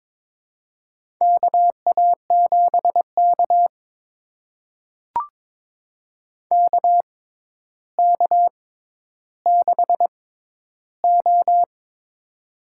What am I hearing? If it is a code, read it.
Morse code, heard as a keyed tone, about 22 wpm, spaced as for 11 wpm, 700 Hz. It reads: KA7K KK6O
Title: USA Call Signs - 4 Characters - 6x Character Spacing - 22wpm